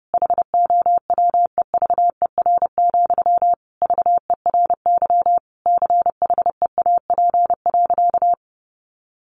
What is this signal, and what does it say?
Morse code, heard as a keyed tone, about 28 wpm, 700 Hz. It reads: HOWEVER, VERY CHEAP.
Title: Sentences from Top 1200 Words 30wpm